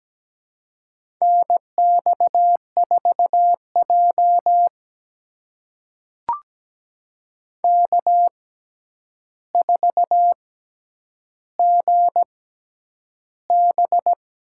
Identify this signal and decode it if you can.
Morse code, heard as a keyed tone, about 17 wpm, spaced as for 9 wpm, 700 Hz. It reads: NX4J K4GB
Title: USA Call Signs - 4 Characters - 6x Character Spacing - 17wpm